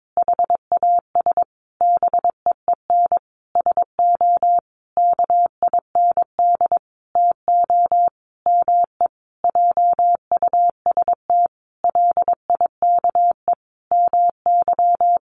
Morse code, heard as a keyed tone, about 22 wpm, 700 Hz. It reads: HAS BEEN SO KIND TO ME JUST LIKE MY
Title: Sentences from Top 1200 Words 22wpm